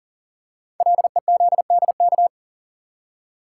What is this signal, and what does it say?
Morse code, heard as a keyed tone, about 39 wpm, 700 Hz. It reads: LEZDK
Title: Code Groups - 5 Letters - 40wpm